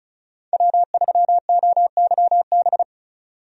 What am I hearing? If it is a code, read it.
Morse code, heard as a keyed tone, about 34 wpm, 700 Hz. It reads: W3OYB
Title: USA Call Signs - 5 Characters - 35wpm